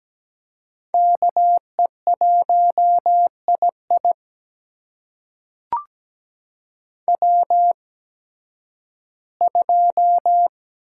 Morse code, heard as a keyed tone, about 17 wpm, spaced as for 11 wpm, 700 Hz. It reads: KE1II W2
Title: USA Call Signs - 5 Characters - 8x Character Spacing - 17wpm